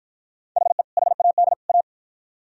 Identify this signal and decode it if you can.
Morse code, heard as a keyed tone, about 48 wpm, 700 Hz. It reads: HE HAD A